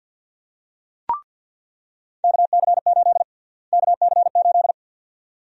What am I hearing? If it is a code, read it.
Morse code, heard as a keyed tone, about 48 wpm, 700 Hz. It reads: KC8 KC8
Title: USA Call Sign Prefixes - Repeated 2x - 50wpm